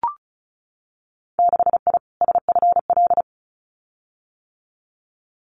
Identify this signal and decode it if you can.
Morse code, heard as a keyed tone, about 34 wpm, 700 Hz. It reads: 6I SFL
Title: Winter Field Day Exchanges 35wpm